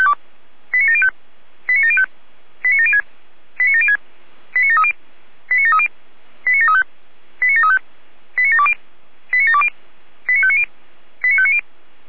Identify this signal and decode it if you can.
Five-tone selective call (ZVEI), heard as quick runs of five tones, tones of about 70 ms. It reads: 89705 89705 89706 89706 89730 89730 89735 89735 89720 89720 89590 89590